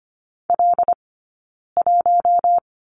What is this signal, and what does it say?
Morse code, heard as a keyed tone, about 25 wpm, 700 Hz. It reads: L 1
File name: Lesson 17d - Letter u - Two Characters 25wpm